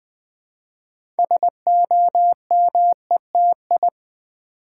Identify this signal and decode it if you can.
Morse code, heard as a keyed tone, about 20 wpm, 700 Hz. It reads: SOMETI
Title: Build A Word - 2 Letters At A Time - 8-Letter Words - 20wpm